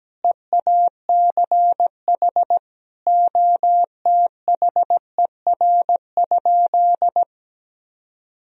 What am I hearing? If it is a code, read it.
Morse code, heard as a keyed tone, about 17 wpm, 700 Hz. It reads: EACH OTHER?
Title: Sentences from Top 1000 Words - Review Part 1 - 17wpm